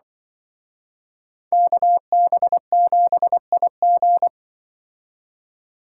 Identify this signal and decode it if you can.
Morse code, heard as a keyed tone, about 24 wpm, 700 Hz. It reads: KB7IG
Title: Speed Racing - USA 2x2 Call Signs 20wpm